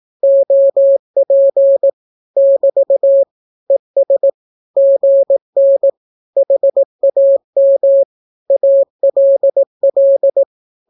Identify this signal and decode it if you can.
Morse code, heard as a keyed tone, about 18 wpm, 550 Hz. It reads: OP <BT> ES GN HAM ALL
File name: naukapodstaw / skroty / skroty